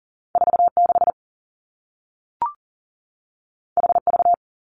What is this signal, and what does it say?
Morse code, heard as a keyed tone, about 42 wpm, 700 Hz. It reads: 4 6 H V
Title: Two Challenging Letter-Number Combinations 40wpm